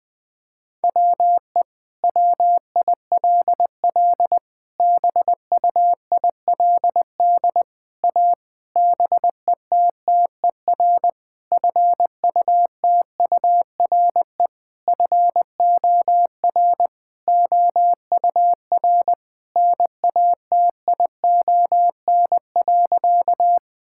Morse code, heard as a keyed tone, about 21 wpm, 700 Hz. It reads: WE WILL BUILD A BETTER FUTURE FOR OUR NATION.
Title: Sentences from Top 1000 Words - Review Part 6 - 20wpm